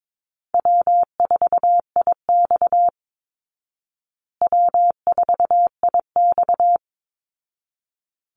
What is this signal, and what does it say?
Morse code, heard as a keyed tone, about 22 wpm, 690 Hz. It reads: W4IX W4IX
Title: Easy Call Signs Repeated 2x - 22wpm